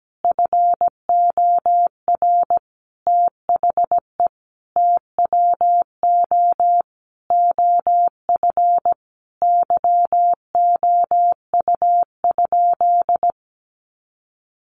Morse code, heard as a keyed tone, about 17 wpm, 700 Hz. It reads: FOR THE TWO OF YOU?
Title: Sentences from Top 1400 Words 17wpm